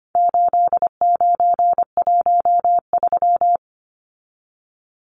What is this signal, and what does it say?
Morse code, heard as a keyed tone, about 25 wpm, 700 Hz. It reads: 8913
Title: Numbers - 4-Digit 25wpm